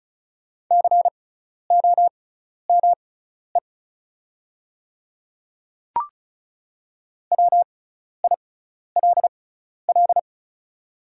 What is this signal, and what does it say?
Morse code, heard as a keyed tone, about 34 wpm, spaced as for 12 wpm, 700 Hz. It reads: COME WILL
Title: 6x Character Spacing - Top 100 Words - 35wpm